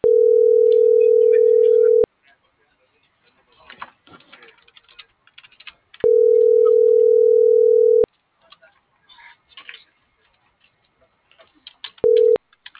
Ringback tone (North America), 440 + 480 Hz.